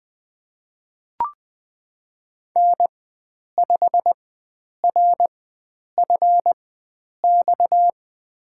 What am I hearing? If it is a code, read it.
Morse code, heard as a keyed tone, about 20 wpm, spaced as for 9 wpm, 700 Hz. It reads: N5RFX
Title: USA Call Signs - 4x Character Spacing - 20wpm